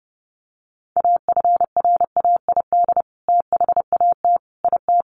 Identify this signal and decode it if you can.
Morse code, heard as a keyed tone, about 30 wpm, 700 Hz. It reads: AFRAID THAT IT
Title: Sentences from Top 1500 Words 30wpm